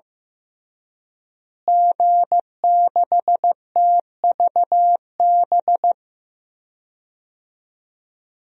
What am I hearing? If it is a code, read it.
Morse code, heard as a keyed tone, about 15 wpm, 700 Hz. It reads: G6TVB
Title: DXCC-100 Call Signs - 5 Characters - Repeated 3x - 15wpm